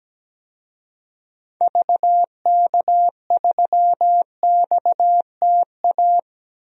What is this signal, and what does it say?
Morse code, heard as a keyed tone, about 17 wpm, 700 Hz. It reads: VK3XTA
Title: DXCC-100 Call Signs - 6 Characters - 17wpm